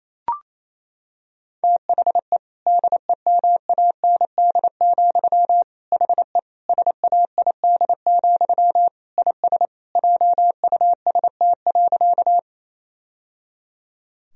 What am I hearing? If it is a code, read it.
Morse code, heard as a keyed tone, about 28 wpm, 700 Hz. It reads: THE DEMAND, HE SAID, IS JUST.
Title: Sentences from Top 1000 Words - Review Part 4 - 28wpm